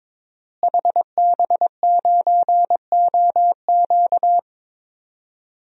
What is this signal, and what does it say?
Morse code, heard as a keyed tone, about 22 wpm, 700 Hz. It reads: HB9OQ
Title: DXCC-100 Call Signs - 5 Characters - Repeated 3x - 22wpm